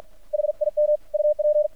Morse code, heard as a keyed tone, about 53 wpm, 600 Hz. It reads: HIM UP
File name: archived / CRSnet